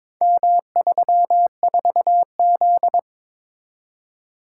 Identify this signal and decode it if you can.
Morse code, heard as a keyed tone, about 22 wpm, 700 Hz. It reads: M34Z
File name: practice / 22wpm / Four Letter-Number Combinations - 22wpm